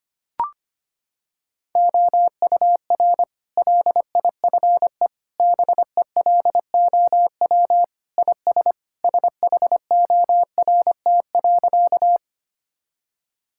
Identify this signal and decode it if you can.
Morse code, heard as a keyed tone, about 25 wpm, 700 Hz. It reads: OUR LIFE BELOW IS SHORT.
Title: Sentences from Top 1000 Words - Review Part 5 - 25wpm